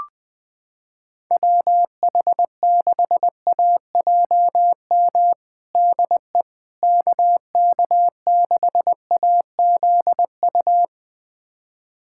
Morse code, heard as a keyed tone, about 20 wpm, 700 Hz. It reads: WH6AJM DE KK6AZU
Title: USA Callsign DE Callsign 20wpm